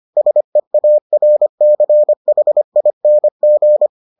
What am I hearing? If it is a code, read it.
Morse code, heard as a keyed tone, about 25 wpm, 600 Hz. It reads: SEARCHING